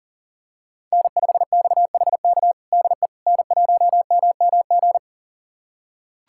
Morse code, heard as a keyed tone, about 40 wpm, 700 Hz. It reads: N5XHK DE N1MMG
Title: USA Callsign DE Callsign - 5 Characters - 40wpm